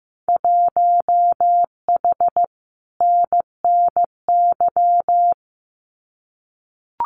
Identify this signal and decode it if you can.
Morse code, heard as a keyed tone, about 15 wpm, 700 Hz. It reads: 1H NNY
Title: Winter Field Day Exchanges 15wpm